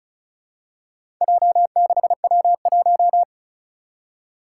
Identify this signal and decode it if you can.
Morse code, heard as a keyed tone, about 36 wpm, 700 Hz. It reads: J6W1
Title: Four Challenging Letter-Number Combinations 35wpm